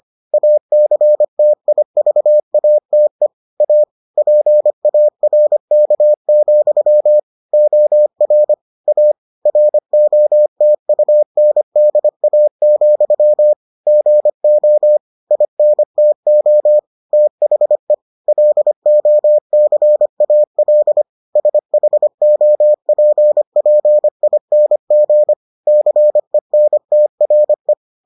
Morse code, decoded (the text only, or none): ACTIVATE A PARK, OR A ROTUNDA, GO INTO THE LOCAL SHOPPING CENTRE